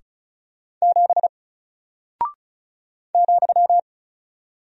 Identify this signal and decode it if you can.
Morse code, heard as a keyed tone, about 36 wpm, 700 Hz. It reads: 7 ,
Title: Single Character (Letter:Number:Punctuation:Prosign) Alt 35wpm